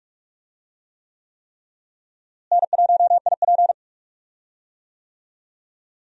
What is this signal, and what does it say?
Morse code, heard as a keyed tone, about 46 wpm, 700 Hz. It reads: N1IP
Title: USA Call Signs - 4 Characters - Repeated 3x - 45wpm